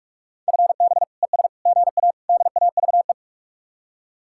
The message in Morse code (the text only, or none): FB ES GA DAVE